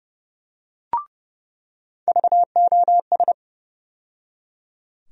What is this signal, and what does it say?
Morse code, heard as a keyed tone, about 30 wpm, 700 Hz. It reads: VOS
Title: USA Call Sign Postfixes - 3 Letters 30wpm